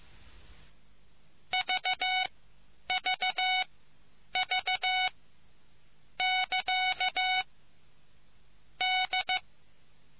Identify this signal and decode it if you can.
Morse code, heard as a keyed tone, about 15 wpm, spaced as for 8 wpm, 2250 Hz. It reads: VVV<KA> D